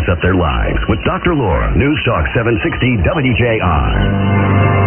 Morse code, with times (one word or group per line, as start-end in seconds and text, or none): none